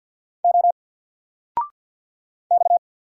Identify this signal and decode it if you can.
Morse code, heard as a keyed tone, about 50 wpm, 700 Hz. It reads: O X
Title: Single Letter-Number 50wpm